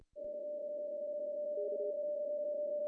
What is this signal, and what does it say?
PSK31, about 31.25 baud.
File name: CW / cqww-cw-k5tr